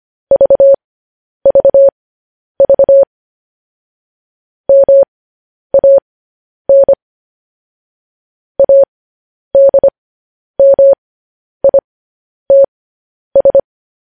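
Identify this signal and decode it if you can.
Morse code, heard as a keyed tone, about 25 wpm, spaced as for 10 wpm, 550 Hz. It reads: VVV MAN ADMITS
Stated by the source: Morse transmission of news headlines.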